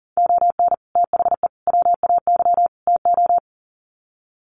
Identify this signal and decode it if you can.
Morse code, heard as a keyed tone, about 40 wpm, 700 Hz. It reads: ON THE WAY TO
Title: Sets of 4 Words 40wpm